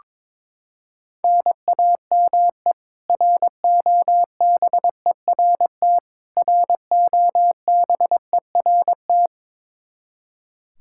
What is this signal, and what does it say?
Morse code, heard as a keyed tone, about 22 wpm, 700 Hz. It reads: NAME ROBERT ROBERT
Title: Practice QSO - Exchange 1 - Beginning - 22wpm